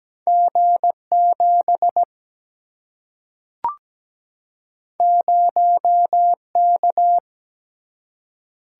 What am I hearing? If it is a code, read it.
Morse code, heard as a keyed tone, about 17 wpm, 700 Hz. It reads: G7 0K
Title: Single-Letter Single-Number 17wpm